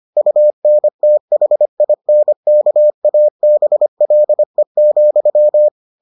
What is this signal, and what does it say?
Morse code, heard as a keyed tone, about 25 wpm, 600 Hz. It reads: UNTHINKABLE,